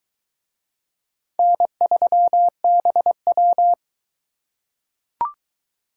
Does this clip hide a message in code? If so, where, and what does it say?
Yes: Morse code from 1.4 to 3.7 s, heard as a keyed tone, about 19 wpm, 700 Hz. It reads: N3BW